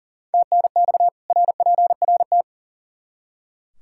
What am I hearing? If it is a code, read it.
Morse code, heard as a keyed tone, about 40 wpm, 700 Hz. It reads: TNX RPRT
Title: K1USN SST QSO Practice 40wpm